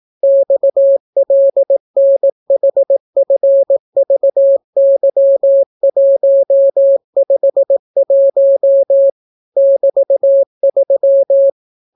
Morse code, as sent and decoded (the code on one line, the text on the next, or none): -..- .-.. -. .... ..-. ...- -.-- .---- ..... .---- / -...- ...--
XLNHFVY151 <BT>3